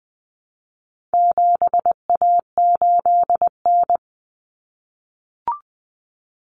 Morse code, heard as a keyed tone, about 20 wpm, 700 Hz. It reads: 7A8N
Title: Common Cut Numbers - 4-Digit 20wpm